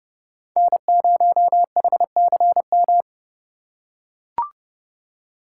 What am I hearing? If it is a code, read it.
Morse code, heard as a keyed tone, about 30 wpm, 700 Hz. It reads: N0HCM
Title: USA Call Signs - 5 Characters - 2x Character Spacing - 30wpm